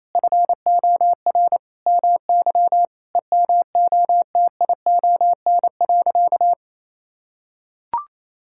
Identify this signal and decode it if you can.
Morse code, heard as a keyed tone, about 28 wpm, 700 Hz. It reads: FOR MY EMOTION.
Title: Sentences from Top 1600 Words 28wpm